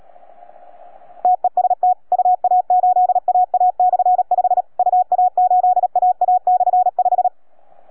Morse code, heard as a keyed tone, about 37 wpm, 730 Hz. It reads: TEST UA8AA/5 UA8AA/5